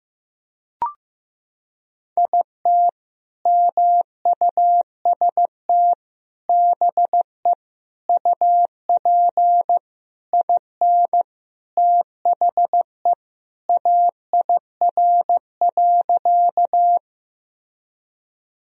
Morse code, heard as a keyed tone, about 15 wpm, 700 Hz. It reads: IT MUST BE UP IN THE AIR.
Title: Sentences from Top 1000 Words - Review Part 1 - 15wpm